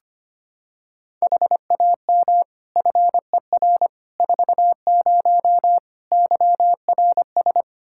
Morse code, heard as a keyed tone, about 25 wpm, 700 Hz. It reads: HAM FER 40 YRS
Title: Practice QSO - Exchange 3 - More Info - 25wpm